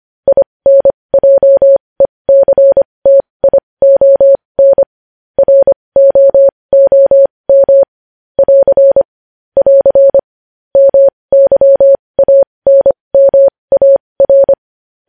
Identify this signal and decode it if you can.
Morse code, heard as a keyed tone, about 25 wpm, spaced as for 20 wpm, 550 Hz. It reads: INJECTION ROOM <AR> <AR> MYANMAR
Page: news-headlines-in-morse-code-at-20-wpm